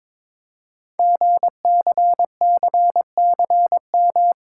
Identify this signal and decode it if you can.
Morse code, heard as a keyed tone, about 23 wpm, 700 Hz. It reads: GCCCM
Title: Code Groups - 5 Letters - 22wpm